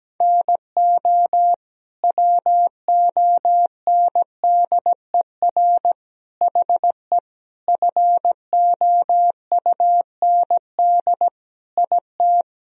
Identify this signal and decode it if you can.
Morse code, heard as a keyed tone, about 17 wpm, 700 Hz. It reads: NO WONDER HE FOUND IT
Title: Sentences from Top 1000 Words - Review Part 3 - 17wpm